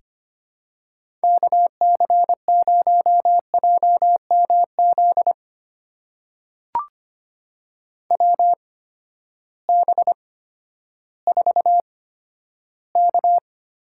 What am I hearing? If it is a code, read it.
Morse code, heard as a keyed tone, about 25 wpm, spaced as for 12 wpm, 700 Hz. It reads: KC0JMZ WB4K
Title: USA Call Signs - 6 Characters - 8x Character Spacing - 25wpm